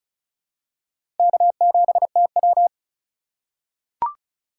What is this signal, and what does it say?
Morse code, heard as a keyed tone, about 36 wpm, 700 Hz. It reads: K7TW
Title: USA Call Signs - 4 Characters - 4x Character Spacing - 35wpm